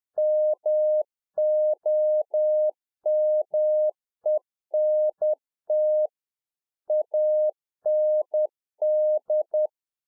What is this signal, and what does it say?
Morse code, heard as a keyed tone, about 10 wpm, 610 Hz. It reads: MOMENT AND